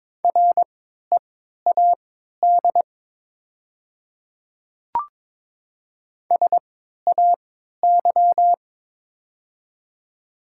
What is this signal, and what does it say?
Morse code, heard as a keyed tone, about 22 wpm, spaced as for 13 wpm, 700 Hz. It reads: READ SAY
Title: 3x Character Spacing - Top 200 Words - 22wpm